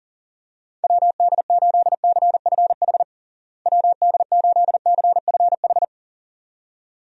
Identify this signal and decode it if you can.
Morse code, heard as a keyed tone, about 40 wpm, 700 Hz. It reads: WD8CFH WD8CFH
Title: USA Call Signs - 6 Characters - Repeated 2x - 40wpm